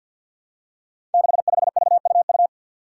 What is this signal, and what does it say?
Morse code, heard as a keyed tone, about 52 wpm, 700 Hz. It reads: 65VUU
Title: Five Challenging Letter-Number Combinations 50wpm